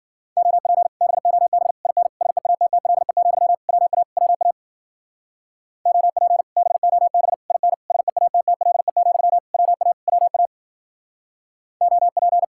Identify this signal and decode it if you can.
Morse code, heard as a keyed tone, about 49 wpm, 700 Hz. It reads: OP BOB IN SEATTLE, WA WA OP BOB IN SEATTLE, WA WA OP
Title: Speed-Racing Common QSO Phrases - 40wpm